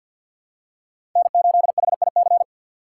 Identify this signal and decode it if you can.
Morse code, heard as a keyed tone, about 49 wpm, 700 Hz. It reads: N8HIC
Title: USA Call Signs - 5 Characters - 4x Character Spacing - 50wpm